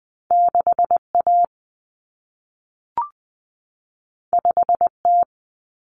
Morse code, heard as a keyed tone, about 20 wpm, 700 Hz. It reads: 6A 5T
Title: Common Cut Numbers - 2-Digit 20wpm